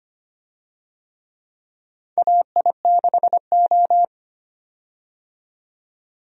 Morse code, heard as a keyed tone, about 25 wpm, 700 Hz. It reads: AI6O
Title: USA Call Signs - 4 Characters - 25wpm